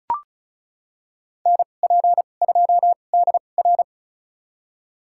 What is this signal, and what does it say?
Morse code, heard as a keyed tone, about 36 wpm, spaced as for 25 wpm, 700 Hz. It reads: NP2DR
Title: USA Call Signs - 5 Characters - 2x Character Spacing - 35wpm